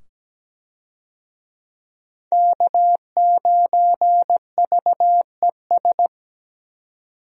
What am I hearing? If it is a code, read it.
Morse code, heard as a keyed tone, about 17 wpm, 700 Hz. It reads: K9VES
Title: USA Call Signs - 5 Characters - 17wpm